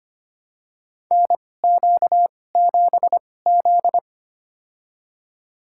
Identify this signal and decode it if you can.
Morse code, heard as a keyed tone, about 25 wpm, spaced as for 18 wpm, 700 Hz. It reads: NQ7Z
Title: USA Call Signs - 4 Characters - 2x Character Spacing - 25wpm